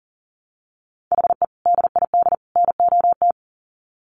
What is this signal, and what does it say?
Morse code, heard as a keyed tone, about 40 wpm, 700 Hz. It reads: HE DID NOT
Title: Sets of 3 Words 40wpm